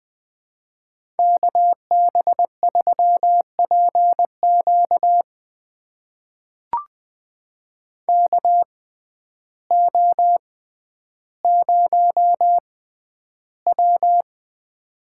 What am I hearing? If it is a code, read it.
Morse code, heard as a keyed tone, about 21 wpm, spaced as for 11 wpm, 700 Hz. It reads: KB3PQ KO0W
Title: USA Call Signs - 6x Character Spacing - 20wpm